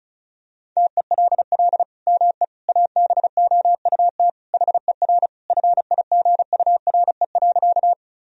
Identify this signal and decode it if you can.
Morse code, heard as a keyed tone, about 36 wpm, 700 Hz. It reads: TELL ME ABOUT HER FIGURE.
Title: Sentences from Top 1000 Words - Review Part 2 - 35wpm